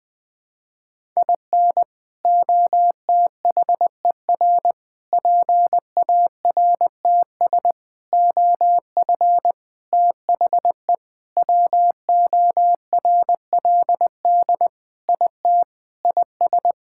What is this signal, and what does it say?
Morse code, heard as a keyed tone, about 20 wpm, 700 Hz. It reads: IN OTHER PARTS OF THE WORLD IT IS
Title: Sentences with 6-Word Phrases - 20wpm